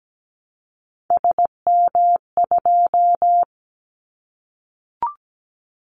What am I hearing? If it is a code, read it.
Morse code, heard as a keyed tone, about 17 wpm, 700 Hz. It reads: SM2
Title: Callsign Prefixes for Top 25 DXCC - 17wpm